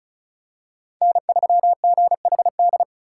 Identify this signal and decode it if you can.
Morse code, heard as a keyed tone, about 36 wpm, 700 Hz. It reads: N3GHD
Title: USA Call Signs - 5 Characters - 3x Character Spacing - 35wpm